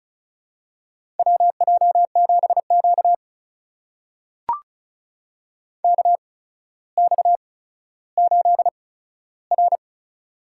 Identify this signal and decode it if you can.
Morse code, heard as a keyed tone, about 36 wpm, spaced as for 15 wpm, 700 Hz. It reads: WJ7Q KX8R
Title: USA Call Signs - 4 Characters - 8x Character Spacing - 35wpm